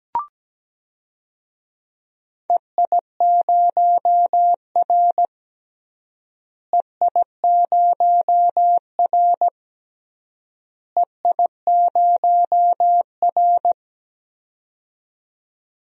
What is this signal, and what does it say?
Morse code, heard as a keyed tone, about 17 wpm, 700 Hz. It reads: EI0R EI0R EI0R
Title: DXCC-100 Call Signs - 4 Characters - Repeated 3x - 17wpm